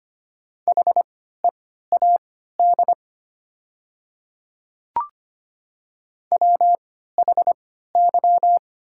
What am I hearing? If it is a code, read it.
Morse code, heard as a keyed tone, about 25 wpm, spaced as for 14 wpm, 700 Hz. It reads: HEAD WHY
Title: 3x Character Spacing - Top 500 Words - 25wpm